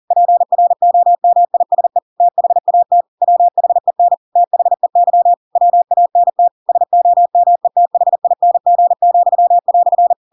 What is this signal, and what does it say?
Morse code, heard as a keyed tone, about 40 wpm, 700 Hz. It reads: PROMISE THAT WHEN THEY WANT SOMETHING,"